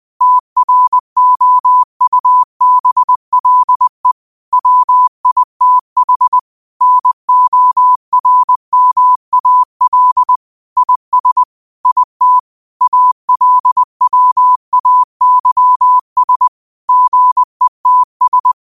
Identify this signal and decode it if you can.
Morse code, heard as a keyed tone, about 20 wpm, 1000 Hz. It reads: TROUBLE WITH NORMAL IS IT ALWAYS GETS
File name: qotdfiles / 20wpm